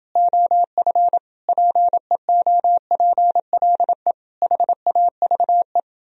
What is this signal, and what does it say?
Morse code, heard as a keyed tone, about 31 wpm, 700 Hz. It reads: OF PEOPLE HAVE